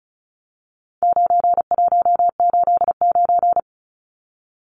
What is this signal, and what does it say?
Morse code, heard as a keyed tone, about 34 wpm, 700 Hz. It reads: 9189
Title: Numbers - 4-Digit 35wpm